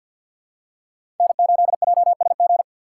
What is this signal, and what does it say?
Morse code, heard as a keyed tone, about 53 wpm, 700 Hz. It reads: N8JSG
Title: USA Call Signs - 5 Characters - 8x Character Spacing - 50wpm